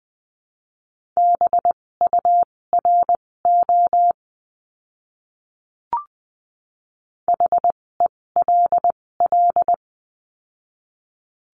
Morse code, heard as a keyed tone, about 20 wpm, spaced as for 16 wpm, 700 Hz. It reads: BURO HELL
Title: 256 Lesson 42a - Words 12-20wpm